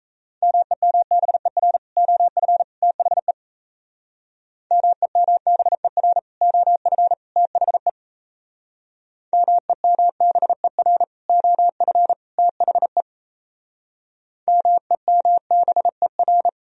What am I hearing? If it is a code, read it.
Morse code, heard as a keyed tone, about 36 wpm, 700 Hz. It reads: MEMBER OF THE MEMBER OF THE MEMBER OF THE MEMBER